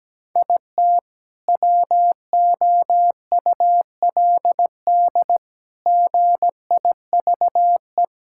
Morse code, heard as a keyed tone, about 17 wpm, 700 Hz. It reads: IT WOULD GIVE
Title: Sentences from Top 1600 Words 17wpm